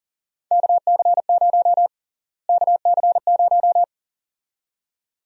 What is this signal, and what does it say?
Morse code, heard as a keyed tone, about 39 wpm, 700 Hz. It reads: KC0 KC0
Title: USA Call Sign Prefixes - Repeated 2x - 40wpm